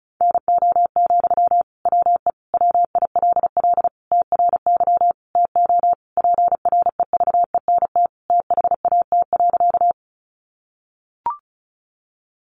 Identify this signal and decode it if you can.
Morse code, heard as a keyed tone, about 34 wpm, 700 Hz. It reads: NO, WE WILL TRY TO PREVENT THAT.
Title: Sentences from Top 1100 Words 35wpm